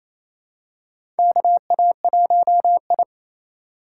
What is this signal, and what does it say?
Morse code, heard as a keyed tone, about 27 wpm, 700 Hz. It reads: KA1I
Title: USA Call Signs - 4 Characters - 4x Character Spacing - 28wpm